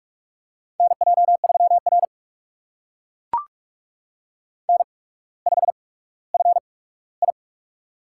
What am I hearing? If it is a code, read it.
Morse code, heard as a keyed tone, about 44 wpm, spaced as for 19 wpm, 700 Hz. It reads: NJ3R N5FI
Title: USA Call Signs - 4 Characters - 8x Character Spacing - 45wpm